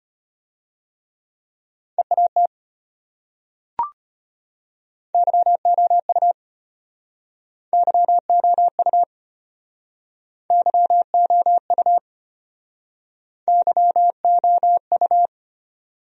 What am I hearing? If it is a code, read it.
Morse code, heard as a keyed tone, about 33 wpm, 700 Hz. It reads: EAT YOU YOU YOU YOU